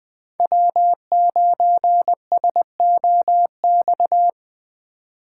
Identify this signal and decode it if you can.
Morse code, heard as a keyed tone, about 20 wpm, 700 Hz. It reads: W9SOX